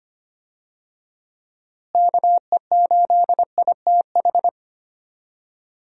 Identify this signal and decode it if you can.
Morse code, heard as a keyed tone, about 25 wpm, 700 Hz. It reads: KE8ITH
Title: USA Call Signs - 6 Characters - Repeated 2x - 25wpm